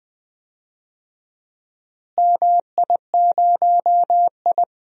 Morse code, heard as a keyed tone, about 20 wpm, 700 Hz. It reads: MI0I
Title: DXCC-100 Call Signs - 4 Characters - Repeated 2x - 20wpm